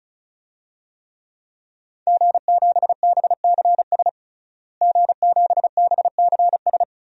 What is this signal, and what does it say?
Morse code, heard as a keyed tone, about 34 wpm, 700 Hz. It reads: G7BCS G7BCS
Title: DXCC-100 Call Signs - 5 Characters - Repeated 3x - 35wpm